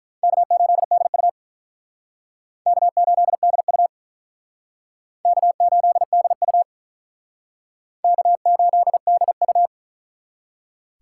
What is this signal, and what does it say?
Morse code, heard as a keyed tone, about 48 wpm, 700 Hz. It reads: K8DU K8DU K8DU K8DU